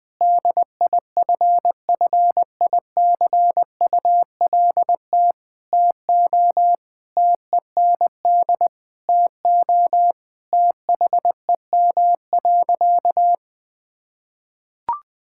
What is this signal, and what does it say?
Morse code, heard as a keyed tone, about 20 wpm, 700 Hz. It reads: DIFFICULT TO TEND TO THEM.
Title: Sentences from Top 1000 Words - Review Part 5 - 20wpm